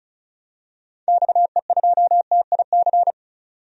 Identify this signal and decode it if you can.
Morse code, heard as a keyed tone, about 34 wpm, 700 Hz. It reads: XE2TIC
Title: DXCC-100 Call Signs - 6 Characters - 35wpm